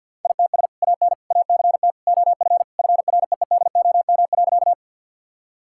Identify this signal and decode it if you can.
Morse code, heard as a keyed tone, about 47 wpm, 700 Hz. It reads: ITS AN ACT OF FREEDOM.
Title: Sentences from Top 1000 Words - Review Part 5 - 50wpm